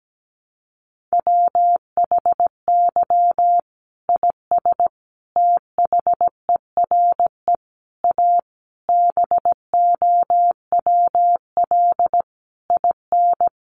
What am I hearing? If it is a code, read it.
Morse code, heard as a keyed tone, about 17 wpm, 700 Hz. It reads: WHY IS THERE A BOWL IN
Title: Sentences from Top 1200 Words 17wpm